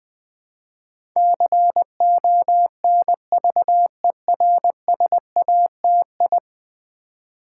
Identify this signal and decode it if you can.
Morse code, heard as a keyed tone, about 20 wpm, 700 Hz. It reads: CONVERSATI
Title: Build A Word - 5 Letters At A Time - 12-Letter Words - 20wpm